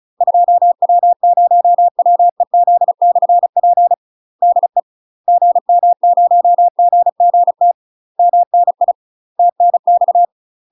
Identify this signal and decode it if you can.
Morse code, heard as a keyed tone, about 34 wpm, 700 Hz. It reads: 2W0WEZ/P DE GM0GGT MNI TNX
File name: UK MASTER QSOs TRAINING / 35WPM / UK MASTER QSOs 0043